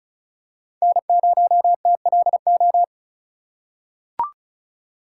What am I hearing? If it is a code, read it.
Morse code, heard as a keyed tone, about 36 wpm, 700 Hz. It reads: N0TLO
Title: USA Call Signs - 5 Characters - 6x Character Spacing - 35wpm